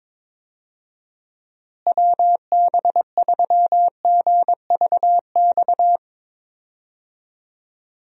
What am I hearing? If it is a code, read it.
Morse code, heard as a keyed tone, about 22 wpm, 700 Hz. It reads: WB3GVX
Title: USA Call Signs - 6 Characters - Repeated 2x - 22wpm